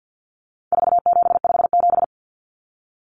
Morse code, heard as a keyed tone, about 50 wpm, 700 Hz. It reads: 4757
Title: Numbers - 4-Digit 50wpm